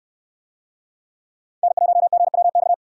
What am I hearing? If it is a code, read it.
Morse code, heard as a keyed tone, about 64 wpm, 700 Hz. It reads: N1GWX